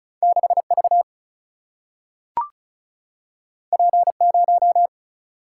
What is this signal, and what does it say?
Morse code, heard as a keyed tone, about 36 wpm, 700 Hz. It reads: 6 V P 0
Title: Single-Letter Single-Number 35wpm